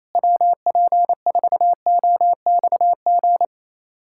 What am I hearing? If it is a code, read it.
Morse code, heard as a keyed tone, about 28 wpm, 700 Hz. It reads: WP4OXG
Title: USA Callsign DE Callsign - 6 Characters - 28wpm